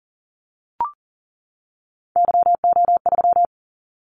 Morse code, heard as a keyed tone, about 40 wpm, 700 Hz. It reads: YO3
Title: Callsign Prefixes for Top 25 DXCC - 40wpm